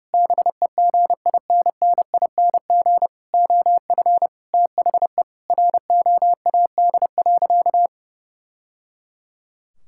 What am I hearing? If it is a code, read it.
Morse code, heard as a keyed tone, about 32 wpm, 700 Hz. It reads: BEGINNING OF THE ROAD.
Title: Sentences from Top 1000 Words - Review Part 5 - 30wpm